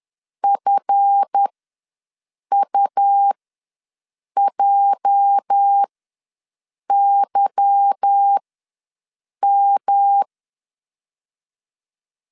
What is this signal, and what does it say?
Morse code, heard as a keyed tone, about 10 wpm, spaced as for 6 wpm, 800 Hz. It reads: FUJYM
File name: LEZIONI / 8 YJ45 / 8 YJ ws 10wpm 4043